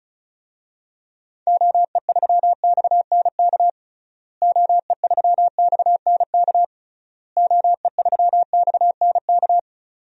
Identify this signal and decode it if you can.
Morse code, heard as a keyed tone, about 35 wpm, 700 Hz. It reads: OE3XNK OE3XNK OE3XNK
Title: DXCC-100 Call Signs - 6 Characters - Repeated 3x - 35wpm